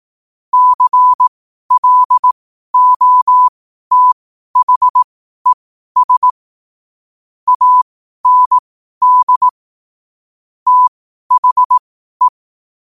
Morse code, heard as a keyed tone, about 18 wpm, spaced as for 12 wpm, 1000 Hz. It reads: CLOTHES AND THE
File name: qotdfiles / 12wpm